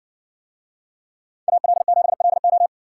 Morse code, heard as a keyed tone, about 60 wpm, 700 Hz. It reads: AL7LO